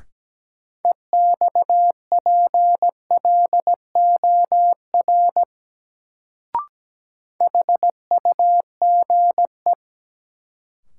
Morse code, heard as a keyed tone, about 17 wpm, 700 Hz. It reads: EXPLOR HUGE
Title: Sentences from Top 1100 Words - New Word Drill - 17wpm